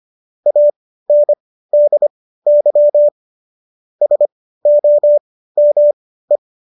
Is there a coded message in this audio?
Yes: Morse code from 0.5 to 6.4 s, heard as a keyed tone, about 25 wpm, spaced as for 15 wpm, 590 Hz. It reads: ANDY SOME